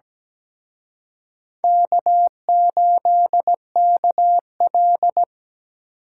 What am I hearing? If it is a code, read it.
Morse code, heard as a keyed tone, about 21 wpm, 700 Hz. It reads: K8KL
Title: Speed Racing - USA 1x2 Call Signs 17wpm